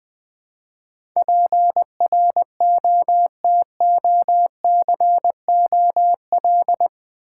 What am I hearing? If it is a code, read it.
Morse code, heard as a keyed tone, about 20 wpm, 700 Hz. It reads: PROTOCOL
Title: Build A Word - 4 Letters At A Time - 8-Letter Words - 20wpm